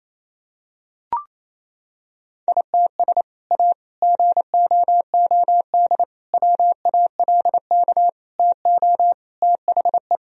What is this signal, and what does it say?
Morse code, heard as a keyed tone, about 28 wpm, 700 Hz. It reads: ITS A GOOD WALK TO THE
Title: Sentences from Top 1000 Words - Review Part 1 - 28wpm